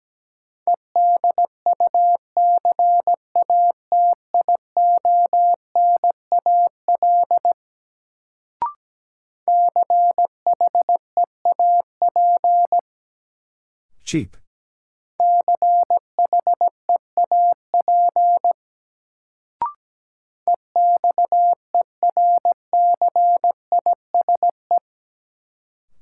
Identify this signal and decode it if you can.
Morse code, heard as a keyed tone, about 17 wpm, 700 Hz. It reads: EDUCATIONAL CHEAP CHEAP EXERCISE
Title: Sentences from Top 1200 Words - New Word Drill Alt - 17wpm